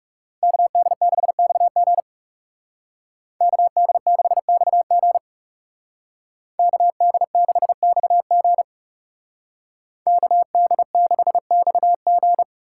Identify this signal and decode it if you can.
Morse code, heard as a keyed tone, about 37 wpm, 700 Hz. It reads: KD6XG KD6XG KD6XG KD6XG
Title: Speed Racing - USA 2x2 Call Signs 30wpm